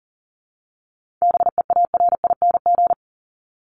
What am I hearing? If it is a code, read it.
Morse code, heard as a keyed tone, about 40 wpm, 700 Hz. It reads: BEARING
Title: ING Suffix - Top 100 Words with Base Words 40wpm